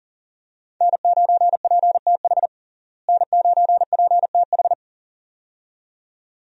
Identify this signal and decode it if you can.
Morse code, heard as a keyed tone, about 40 wpm, 700 Hz. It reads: N9PTH N9PTH
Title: USA Call Signs - 5 Characters - Repeated 2x - 40wpm